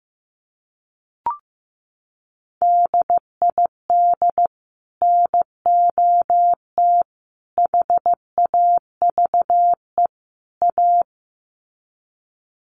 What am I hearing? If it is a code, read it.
Morse code, heard as a keyed tone, about 15 wpm, 700 Hz. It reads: DID NOT HAVE A
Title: Sets of 4 Words 15wpm